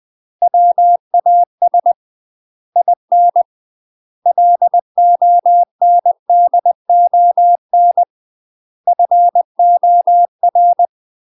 Morse code, heard as a keyed tone, about 20 wpm, 700 Hz. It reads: WAS IN LONDON FOR
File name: UK MASTER QSOs TRAINING / 20WPM / UK MASTER QSOs 0037